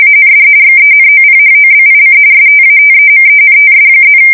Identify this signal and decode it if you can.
RTTY, about 45.45 baud. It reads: AND XENOPHOBIC STATEMENTS+
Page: 170 Shift 2125/2295